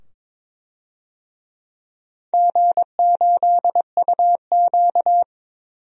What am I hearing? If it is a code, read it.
Morse code, heard as a keyed tone, about 22 wpm, 700 Hz. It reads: G8UQ
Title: DXCC-100 Call Signs - 4 Characters - 22wpm